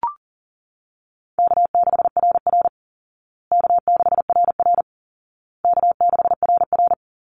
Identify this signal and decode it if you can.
Morse code, heard as a keyed tone, about 40 wpm, 700 Hz. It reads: K6RR K6RR K6RR
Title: Call Signs Repeated 2x - 40wpm